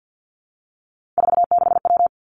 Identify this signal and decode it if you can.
Morse code, heard as a keyed tone, about 50 wpm, 700 Hz. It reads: 46W
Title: Three Challenging Letter-Number Combinations 50wpm